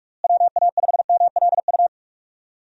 Morse code, heard as a keyed tone, about 42 wpm, 700 Hz. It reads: WA5MLU
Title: USA Call Signs - 6 Characters - Repeated 3x - 45wpm